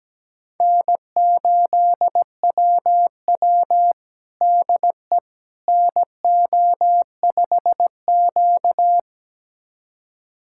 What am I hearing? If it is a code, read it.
Morse code, heard as a keyed tone, about 17 wpm, 700 Hz. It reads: N8WW DE NO5Q
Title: USA Callsign DE Callsign - 4 Characters - 17wpm